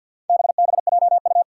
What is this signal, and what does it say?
Morse code, heard as a keyed tone, about 51 wpm, 700 Hz. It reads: BBJU
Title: Four Challenging Letter-Number Combinations 50wpm